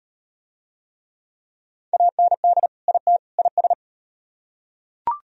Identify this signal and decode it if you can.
Morse code, heard as a keyed tone, about 33 wpm, 700 Hz. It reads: AND IT IS